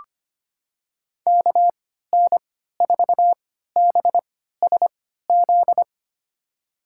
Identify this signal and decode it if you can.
Morse code, heard as a keyed tone, about 25 wpm, spaced as for 14 wpm, 700 Hz. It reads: KN4BSZ
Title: USA Call Signs - 6 Characters - 3x Character Spacing - 25wpm